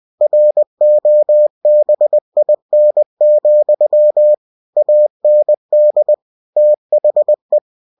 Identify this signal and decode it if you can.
Morse code, heard as a keyed tone, about 20 wpm, 600 Hz. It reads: ROBIN, AND THE
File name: cw-books / winnie-the-pooh / winnie-20wpm-page-0100